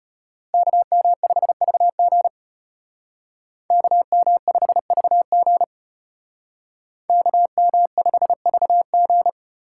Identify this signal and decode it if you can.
Morse code, heard as a keyed tone, about 31 wpm, 700 Hz. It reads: KM5VG KM5VG KM5VG
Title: Speed Racing - USA 2x2 Call Signs 25wpm